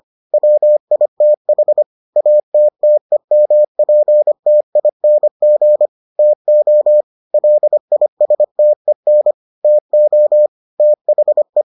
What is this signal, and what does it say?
Morse code, heard as a keyed tone, about 25 wpm, 600 Hz. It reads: WITH ATTEMPTING TO LISTEN TO THE